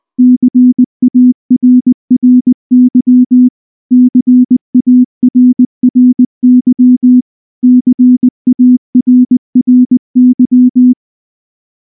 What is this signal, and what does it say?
Morse code, heard as a keyed tone, about 19 wpm, 250 Hz. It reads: CARRY CARRY CARRY
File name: morse / morse-training-20WPM-4